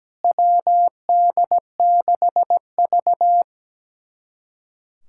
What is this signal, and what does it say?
Morse code, heard as a keyed tone, about 17 wpm, 700 Hz. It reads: WD6V
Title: USA Callsign DE Callsign - 4 Characters - 17wpm